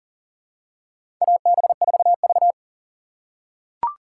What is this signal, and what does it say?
Morse code, heard as a keyed tone, about 40 wpm, 700 Hz. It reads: AB4V
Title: USA Call Signs - 2x Character Spacing - 40wpm